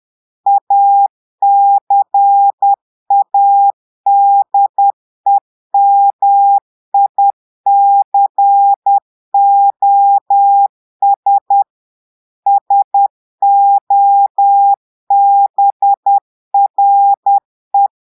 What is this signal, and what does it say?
Morse code, heard as a keyed tone, about 10 wpm, 800 Hz. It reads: ACADEMICOS SOBRE
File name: Portugais - texte en clair 1 - vitesse - 10 mots minute